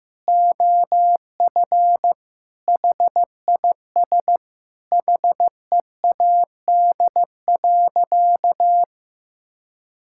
Morse code, heard as a keyed tone, about 15 wpm, 700 Hz. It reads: OF HIS HEAD.
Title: Sentences from Top 1600 Words 15wpm